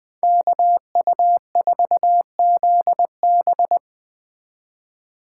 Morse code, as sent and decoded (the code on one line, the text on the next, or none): -.- ..- ....- --.. -...
KU4ZB